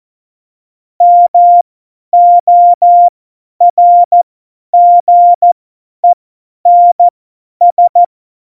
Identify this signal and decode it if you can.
Morse code, heard as a keyed tone, about 14 wpm, spaced as for 10 wpm, 700 Hz. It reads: MORGENS
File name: Morsekurs / Witze / Witz 20. August 2019, 50 F70